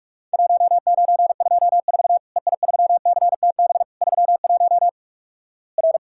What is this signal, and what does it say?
Morse code, heard as a keyed tone, about 45 wpm, 700 Hz. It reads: 1924 EI3CTB 31 R
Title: Worked All Europe DX Contest - QTC Exchange - 45wpm